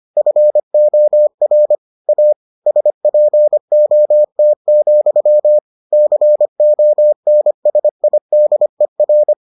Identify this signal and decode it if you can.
Morse code, heard as a keyed tone, about 25 wpm, 600 Hz. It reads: FOR A SPOT, CONSIDER